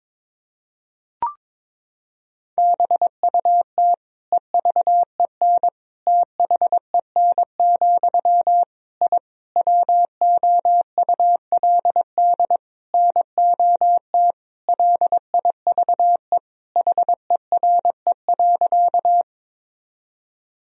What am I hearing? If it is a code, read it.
Morse code, heard as a keyed tone, about 22 wpm, 700 Hz. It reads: BUT EVEN THEN, I WOULD NOT LIVE HERE.
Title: Sentences from Top 1000 Words - Review Part 1 - 22wpm